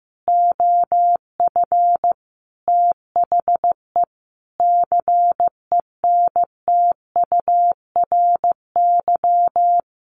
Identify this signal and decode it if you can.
Morse code, heard as a keyed tone, about 15 wpm, 700 Hz. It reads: OF THE CENTURY
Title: Sets of 5 Words 15wpm